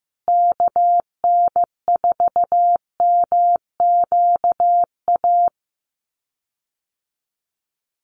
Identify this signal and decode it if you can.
Morse code, heard as a keyed tone, about 15 wpm, 700 Hz. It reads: KN4MQA
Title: USA Call Signs Repeated 3x 15wpm